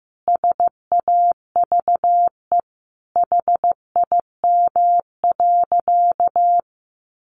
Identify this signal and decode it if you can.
Morse code, heard as a keyed tone, about 15 wpm, 700 Hz. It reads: SAVE HIM.
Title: Sentences from Top 1100 Words 15wpm